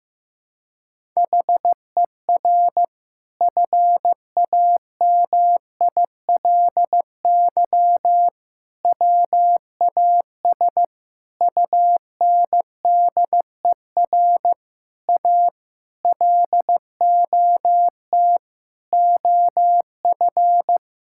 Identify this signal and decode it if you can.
Morse code, heard as a keyed tone, about 16 wpm, 700 Hz. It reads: HER FAMILY WAS UNDER A LOT OF
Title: Sentences from Top 1000 Words - Review Part 6 - 15wpm